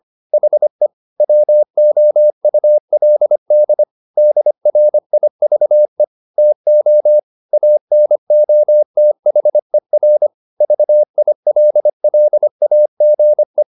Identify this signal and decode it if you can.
Morse code, heard as a keyed tone, about 25 wpm, 600 Hz. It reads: HE WOULD DRIVE TO ANOTHER VILLAGE